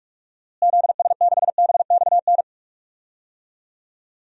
Morse code, heard as a keyed tone, about 46 wpm, 700 Hz. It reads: ZS6BXN
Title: DXCC-100 Call Signs - 6 Characters - Repeated 3x - 45wpm